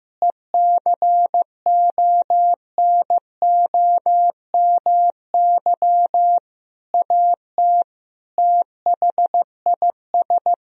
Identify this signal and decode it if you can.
Morse code, heard as a keyed tone, about 15 wpm, 700 Hz. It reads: ECONOMY AT THIS
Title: Sentences from Top 1000 Words - Review Part 3 - 15wpm